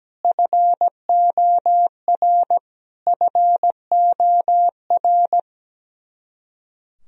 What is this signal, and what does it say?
Morse code, heard as a keyed tone, about 17 wpm, 700 Hz. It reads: FOR FOR
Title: Top 100 Words Repeated 17wpm